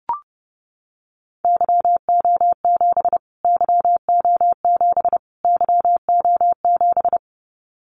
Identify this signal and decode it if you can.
Morse code, heard as a keyed tone, about 30 wpm, 700 Hz. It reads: YO7 YO7 YO7
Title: Callsign Prefixes for Top 25 DXCC - Repeated 3x - 30wpm